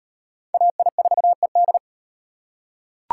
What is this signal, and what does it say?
Morse code, heard as a keyed tone, about 34 wpm, 700 Hz. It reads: AI4ED